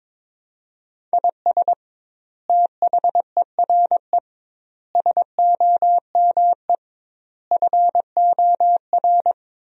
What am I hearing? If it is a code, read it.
Morse code, heard as a keyed tone, about 22 wpm, 700 Hz. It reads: IS THERE SOME FOR
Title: 2x Word-Spacing - Sentences from Top 200 Words - 22wpm